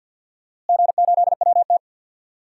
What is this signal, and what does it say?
Morse code, heard as a keyed tone, about 50 wpm, 700 Hz. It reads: G8WT
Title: DXCC-100 Call Signs - 4 Characters - Repeated 2x - 50wpm